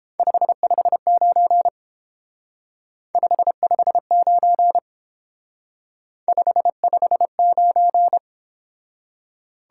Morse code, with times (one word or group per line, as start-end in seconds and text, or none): 0.2-1.7 559
3.1-4.8 559
6.3-8.2 559